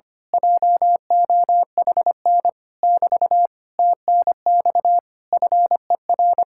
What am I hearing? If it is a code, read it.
Morse code, heard as a keyed tone, about 24 wpm, 700 Hz. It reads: JOHN <BT> TNX FER
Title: Speed-Racing Common QSO Phrases - 25wpm